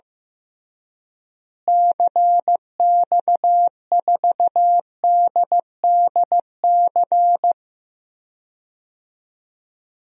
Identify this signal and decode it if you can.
Morse code, heard as a keyed tone, about 15 wpm, 700 Hz. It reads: CX4DDC
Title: DXCC-100 Call Signs - 6 Characters - Repeated 3x - 15wpm